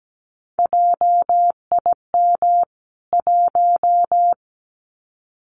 Morse code, heard as a keyed tone, about 17 wpm, 700 Hz. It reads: JIM 1
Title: CWT Exchange 17wpm